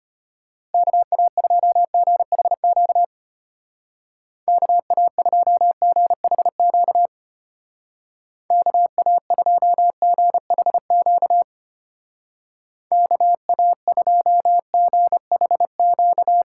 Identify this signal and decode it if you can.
Morse code, heard as a keyed tone, about 30 wpm, 700 Hz. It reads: KA2GHQ KA2GHQ KA2GHQ KA2GHQ
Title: Speed Racing - USA 2x3 Call Signs 25wpm